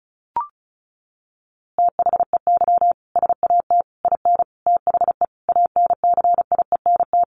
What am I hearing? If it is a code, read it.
Morse code, heard as a keyed tone, about 34 wpm, 700 Hz. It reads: THEY SAT IN THE ANCIENT
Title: A Sound Of Thunder 35wpm